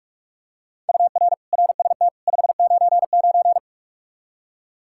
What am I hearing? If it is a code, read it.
Morse code, heard as a keyed tone, about 45 wpm, 700 Hz. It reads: UR RST 599
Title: Practice QSO - Exchange 1 - Beginning - 45wpm